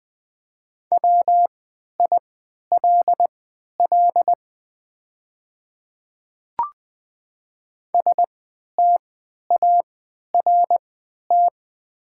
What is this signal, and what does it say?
Morse code, heard as a keyed tone, about 20 wpm, spaced as for 11 wpm, 690 Hz. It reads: WILL START
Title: 3x Character Spacing - Top 300 Words - 20wpm